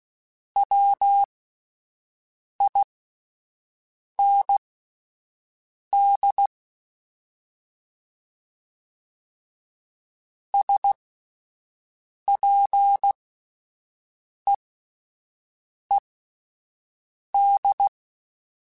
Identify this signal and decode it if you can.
Morse code, heard as a keyed tone, about 16 wpm, spaced as for 5 wpm, 800 Hz. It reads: WIND SPEED
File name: Spaced Characters / Spaced Characters-mp3s / Weather5 12 sp @10